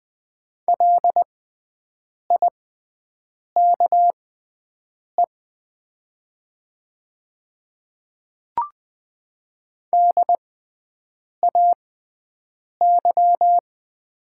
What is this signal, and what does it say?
Morse code, heard as a keyed tone, about 20 wpm, spaced as for 7 wpm, 700 Hz. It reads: LIKE DAY
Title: 6x Character Spacing - Top 100 Words - 20wpm